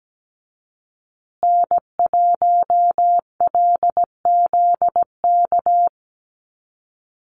Morse code, heard as a keyed tone, about 17 wpm, 700 Hz. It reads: N1LZK
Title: USA Call Signs Repeated 2x 17wpm